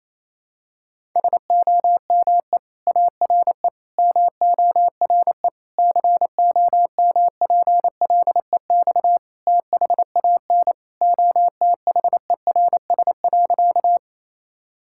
Morse code, heard as a keyed tone, about 28 wpm, 700 Hz. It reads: SOME ARE MORE COMPLEX THAN OTHERS.
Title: Sentences from Top 1000 Words - Review Part 6 - 28wpm